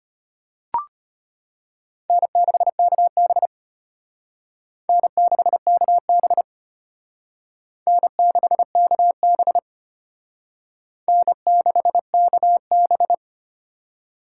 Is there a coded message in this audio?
Yes: Morse code from 2.1 to 13.1 s, heard as a keyed tone, about 32 wpm, 700 Hz. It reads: N6KB N6KB N6KB N6KB